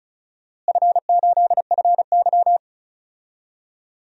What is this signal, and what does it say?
Morse code, heard as a keyed tone, about 34 wpm, 700 Hz. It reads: F8FY
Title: DXCC-100 Call Signs - 4 Characters - Repeated 2x - 35wpm